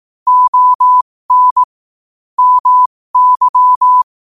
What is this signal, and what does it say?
Morse code, heard as a keyed tone, about 18 wpm, spaced as for 15 wpm, 1000 Hz. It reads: ON MY
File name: qotdfiles / 15wpm